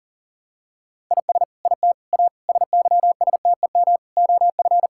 Morse code, heard as a keyed tone, about 40 wpm, 700 Hz. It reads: IS IT A SYSTEM OF
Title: Sentences from Top 1000 Words - Review Part 6 - 40wpm